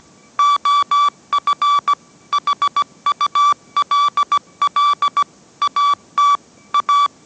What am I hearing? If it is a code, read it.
Morse code, heard as a keyed tone, about 17 wpm, 1150 Hz. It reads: OF HULL AT A